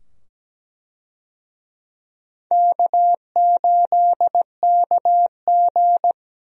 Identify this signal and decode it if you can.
Morse code, heard as a keyed tone, about 17 wpm, 700 Hz. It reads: K8KG
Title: USA Call Signs - 4 Characters - 17wpm